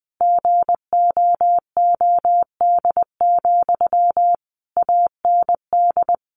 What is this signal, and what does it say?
Morse code, heard as a keyed tone, about 20 wpm, 700 Hz. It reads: GOOD, AND
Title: Sentences from Top 1400 Words 20wpm